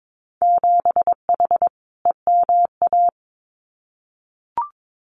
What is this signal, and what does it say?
Morse code, heard as a keyed tone, about 22 wpm, 700 Hz. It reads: 7H EMA
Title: Winter Field Day Exchanges Encore 22wpm